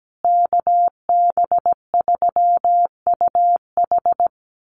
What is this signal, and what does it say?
Morse code, heard as a keyed tone, about 17 wpm, 700 Hz. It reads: KB3UH
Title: USA Call Signs Repeated 3x 17wpm